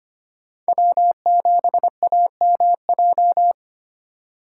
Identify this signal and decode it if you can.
Morse code, heard as a keyed tone, about 25 wpm, 700 Hz. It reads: W7AMJ
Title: USA Call Signs - 5 Characters - 3x Character Spacing - 25wpm